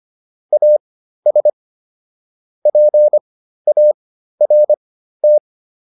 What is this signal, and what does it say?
Morse code, heard as a keyed tone, about 25 wpm, spaced as for 13 wpm, 600 Hz. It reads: AS PART